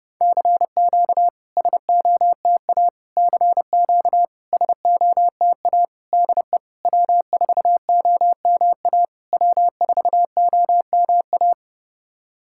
Morse code, heard as a keyed tone, about 31 wpm, 700 Hz. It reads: CQ SOTA CQ SOTA DE W4OMA W4OMA
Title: SOTA QSO Practice 30wpm